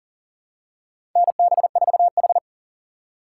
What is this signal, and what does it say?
Morse code, heard as a keyed tone, about 37 wpm, 700 Hz. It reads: NB4H
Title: USA Call Signs - 4 Characters - 3x Character Spacing - 40wpm